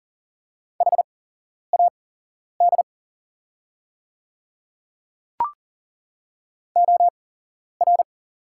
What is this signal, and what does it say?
Morse code, heard as a keyed tone, about 37 wpm, spaced as for 11 wpm, 690 Hz. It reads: HAD OR
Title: 8x Character Spacing - Top 100 Words - 40wpm